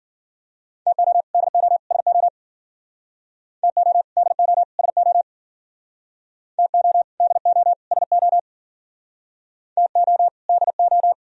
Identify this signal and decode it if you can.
Morse code, heard as a keyed tone, about 51 wpm, 700 Hz. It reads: TO DO SO TO DO SO TO DO SO TO DO